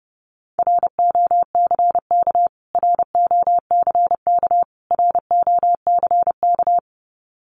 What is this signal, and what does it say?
Morse code, heard as a keyed tone, about 28 wpm, 700 Hz. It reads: ROCK ROCK ROCK
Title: Top 500 Words Repeated 30wpm